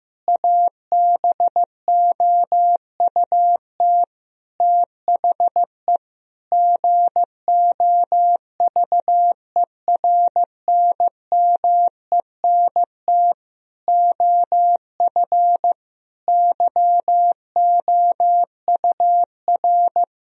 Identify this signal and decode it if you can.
Morse code, heard as a keyed tone, about 15 wpm, 700 Hz. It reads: ABOUT THE GOVERNMENT OF YOUR
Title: Sentences from Top 1000 Words - Review Part 3 - 15wpm